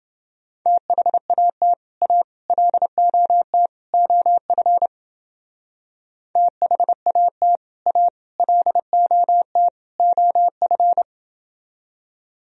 Morse code, heard as a keyed tone, about 28 wpm, 700 Hz. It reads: THAT A LOT OF THAT A LOT OF